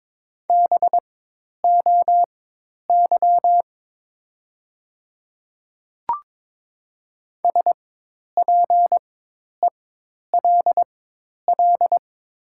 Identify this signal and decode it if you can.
Morse code, heard as a keyed tone, about 22 wpm, spaced as for 10 wpm, 700 Hz. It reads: BOY SPELL
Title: 4x Character Spacing - Top 200 Words - 22wpm